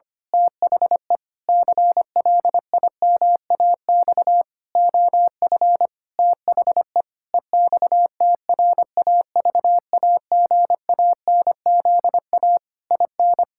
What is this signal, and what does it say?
Morse code, heard as a keyed tone, about 25 wpm, 700 Hz. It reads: THE CLIMAX OF THE EXTRAVAGANZA IN
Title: Pangram Challenge v2 - 25wpm